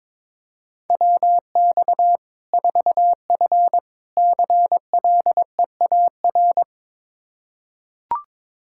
Morse code, heard as a keyed tone, about 21 wpm, 700 Hz. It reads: WX 4F CLEAR
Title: Practice QSO - Exchange 2 - Info - 22wpm